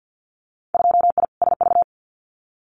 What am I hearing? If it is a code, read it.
Morse code, heard as a keyed tone, about 50 wpm, 700 Hz. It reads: 2I SV
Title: Winter Field Day Exchanges Encore 50wpm